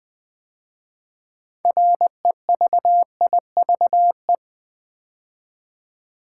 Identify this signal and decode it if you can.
Morse code, heard as a keyed tone, about 20 wpm, 700 Hz. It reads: REVIVE
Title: 6 Letter Words - First 3 Letters Given - 20wpm